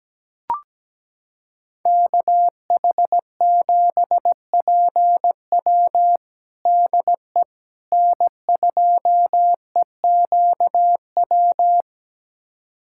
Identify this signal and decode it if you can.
Morse code, heard as a keyed tone, about 17 wpm, 700 Hz. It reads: KH7PW DE N2EQW
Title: USA Callsign DE Callsign - 5 Characters - 17wpm